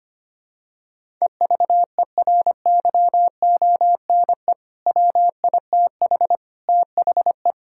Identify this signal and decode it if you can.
Morse code, heard as a keyed tone, about 25 wpm, 700 Hz. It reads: EVERYONE WITH THE
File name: practice / 25wpm / Sentences with 4-Word Phrases - 25wpm